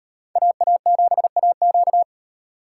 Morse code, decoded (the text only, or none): AA7AQ